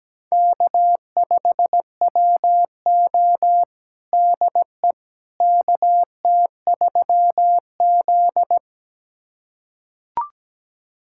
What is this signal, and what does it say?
Morse code, heard as a keyed tone, about 17 wpm, 700 Hz. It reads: K5WO DE KT3Z
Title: USA Callsign DE Callsign - 4 Characters - 17wpm